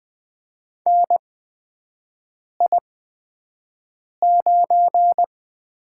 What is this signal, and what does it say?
Morse code, heard as a keyed tone, about 20 wpm, spaced as for 5 wpm, 700 Hz. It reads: NI9
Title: USA Call Signs - 8x Character Spacing - 20wpm